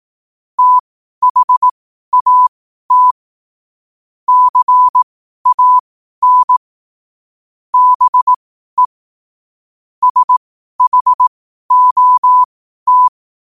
Morse code, heard as a keyed tone, about 18 wpm, spaced as for 12 wpm, 1000 Hz. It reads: THAT CAN BE SHOT
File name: qotdfiles / 12wpm